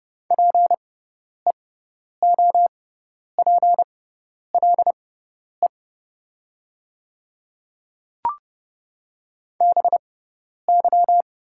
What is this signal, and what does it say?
Morse code, heard as a keyed tone, about 30 wpm, spaced as for 10 wpm, 700 Hz. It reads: PEOPLE BY
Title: 6x Character Spacing - Top 300 Words - 30wpm